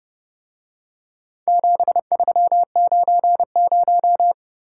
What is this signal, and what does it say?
Morse code, heard as a keyed tone, about 30 wpm, 700 Hz. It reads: 7390